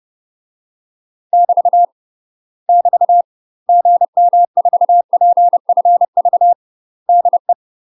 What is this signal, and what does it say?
Morse code, heard as a keyed tone, about 30 wpm, 700 Hz. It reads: <BT> <BT> GM4PFV DE
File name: UK MASTER QSOs TRAINING / 30WPM / UK MASTER QSOs 0016